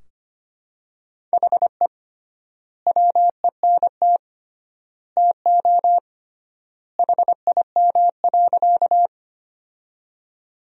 Morse code, heard as a keyed tone, about 25 wpm, 700 Hz. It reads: HE WENT TO HIM.
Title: 3x Word-Spacing - Sentences from Top 200 Words - 25wpm